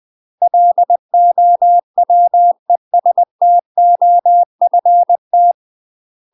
Morse code, heard as a keyed tone, about 20 wpm, 700 Hz. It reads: LOWESTOFT
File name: UK MASTER QSOs TRAINING / 20WPM / UK MASTER QSOs 0072